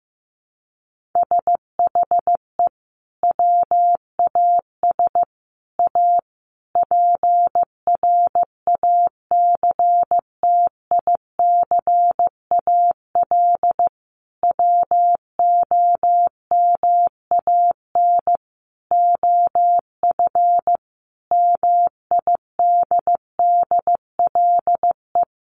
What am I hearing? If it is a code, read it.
Morse code, heard as a keyed tone, about 15 wpm, 700 Hz. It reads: SHE WAS A PRACTICAL WOMAN OF MIDDLE
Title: Sentences from Top 1400 Words 15wpm